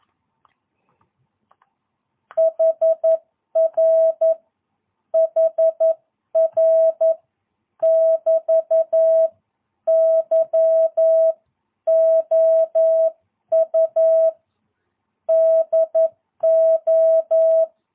Morse code, heard as a keyed tone, about 11 wpm, 650 Hz. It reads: HR HR <BT> YOU DO
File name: jh8cbh / 2022cw